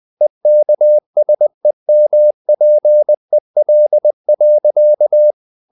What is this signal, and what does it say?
Morse code, heard as a keyed tone, about 20 wpm, 600 Hz. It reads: EKSEMPEL.
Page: Ferie 20 wpm